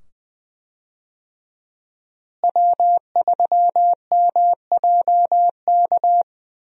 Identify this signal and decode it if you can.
Morse code, heard as a keyed tone, about 21 wpm, 700 Hz. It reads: W3MJK
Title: USA Call Signs - 5 Characters - Repeated 2x - 20wpm